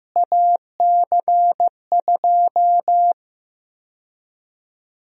Morse code, heard as a keyed tone, about 15 wpm, 700 Hz. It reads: AC2
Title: USA Call Sign Prefixes - Repeated 2x - 15wpm